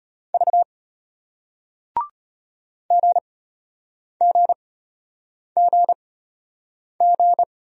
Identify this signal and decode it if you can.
Morse code, heard as a keyed tone, about 34 wpm, spaced as for 8 wpm, 700 Hz. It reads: V GGGG